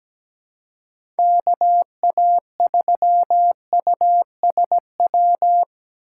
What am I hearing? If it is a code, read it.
Morse code, heard as a keyed tone, about 17 wpm, 700 Hz. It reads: KA3USW
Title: USA Call Signs - 6 Characters - 4x Character Spacing - 17wpm